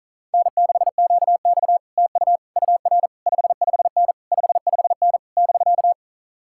Morse code, heard as a keyed tone, about 47 wpm, 700 Hz. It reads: N6QX TU UR 55N 55N <BK>